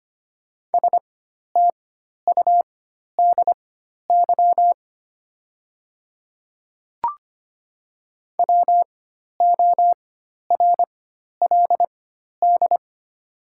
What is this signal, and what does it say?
Morse code, heard as a keyed tone, about 25 wpm, spaced as for 12 wpm, 700 Hz. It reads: STUDY WORLD
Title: 4x Character Spacing - Top 300 Words - 25wpm